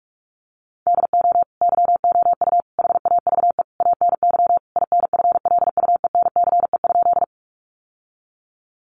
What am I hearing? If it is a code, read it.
Morse code, heard as a keyed tone, about 44 wpm, 700 Hz. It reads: DO YOU HAVE ANY INFLUENCE?
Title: Sentences from Top 1100 Words 45wpm